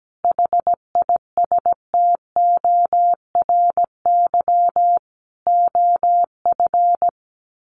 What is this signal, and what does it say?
Morse code, heard as a keyed tone, about 17 wpm, 700 Hz. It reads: HISTORY OF
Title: Sets of 4 Words 17wpm